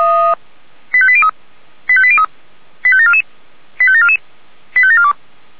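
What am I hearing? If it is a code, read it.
Five-tone selective call (ZVEI), heard as quick runs of five tones, tones of about 70 ms. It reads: 85803 85803 85744 85744 85742
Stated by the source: ZVEI five-tone selective call received with an SDR.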